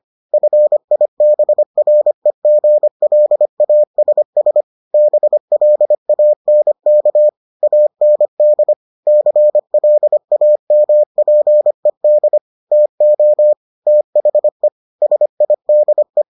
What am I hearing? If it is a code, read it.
Morse code, heard as a keyed tone, about 25 wpm, 600 Hz. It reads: FIBREGLASS BLANK AND CLAMPED TO THE SIDE